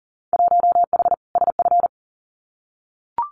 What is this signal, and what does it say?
Morse code, heard as a keyed tone, about 40 wpm, 700 Hz. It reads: 1H SF
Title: Winter Field Day Exchanges 40wpm